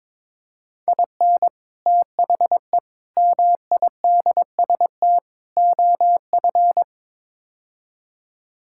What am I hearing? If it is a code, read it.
Morse code, heard as a keyed tone, about 21 wpm, 700 Hz. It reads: IN THE MIDST OF
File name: Build A Phrase - 5-Word Phrases - 22wpm